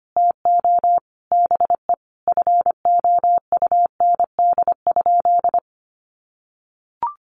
Morse code, heard as a keyed tone, about 25 wpm, 700 Hz. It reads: TO BE FOUND?
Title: Sentences from Top 300 Words 25wpm